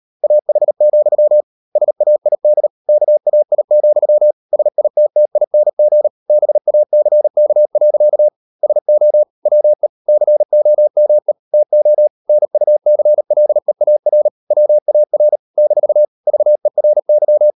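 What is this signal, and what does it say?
Morse code, heard as a keyed tone, about 38 wpm, 600 Hz. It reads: AH, SAID KAI, SITTING BACK. SO WE COME TO NUCLEAR WAR - VERY
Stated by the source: Morse code practice at 35–39 wpm sending English text of Ken Follett's novel Never, from CW med Gnister.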